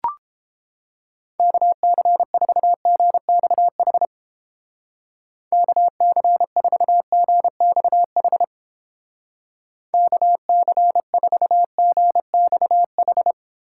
Morse code, heard as a keyed tone, about 29 wpm, 700 Hz. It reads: KC4GXH KC4GXH KC4GXH